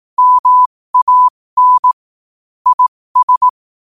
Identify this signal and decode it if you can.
Morse code, heard as a keyed tone, about 18 wpm, spaced as for 15 wpm, 1000 Hz. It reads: MAN IS
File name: qotdfiles / 15wpm